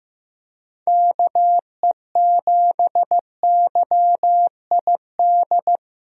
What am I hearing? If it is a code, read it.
Morse code, heard as a keyed tone, about 15 wpm, 700 Hz. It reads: KE7YID
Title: USA Call Signs - 6 Characters - Repeated 3x - 15wpm